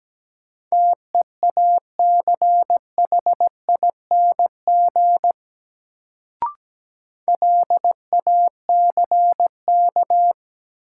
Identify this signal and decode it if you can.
Morse code, heard as a keyed tone, about 17 wpm, 700 Hz. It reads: TEACHING LACK
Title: Sentences from Top 1100 Words - New Word Drill - 17wpm